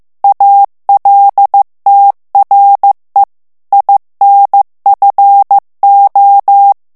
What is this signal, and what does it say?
Morse code, heard as a keyed tone, about 15 wpm, 800 Hz. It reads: ALTRE INFO
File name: morse mp3 / ITALIEN texte clair 1 vitesse 14 mots minute